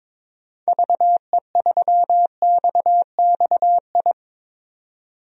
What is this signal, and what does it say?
Morse code, heard as a keyed tone, about 22 wpm, 700 Hz. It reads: VE3XXI
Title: DXCC-100 Call Signs - 6 Characters - 22wpm